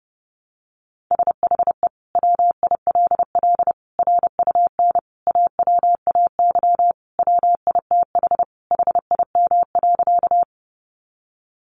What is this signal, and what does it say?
Morse code, heard as a keyed tone, about 30 wpm, 700 Hz. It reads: SHE WILL RUN AWAY WITH HIM.
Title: Sentences from Top 600 Words 30wpm